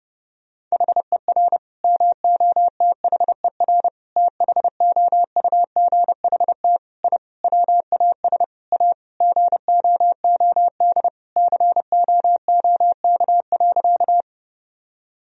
Morse code, heard as a keyed tone, about 32 wpm, 700 Hz. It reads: HER MOTHER THOUGHT I WAS A GOOD COOK.
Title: Sentences from Top 1000 Words - Review Part 4 - 30wpm